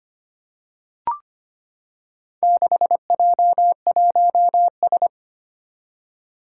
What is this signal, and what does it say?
Morse code, heard as a keyed tone, about 25 wpm, 700 Hz. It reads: 6J1S
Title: Four Challenging Letter-Number Combinations 25wpm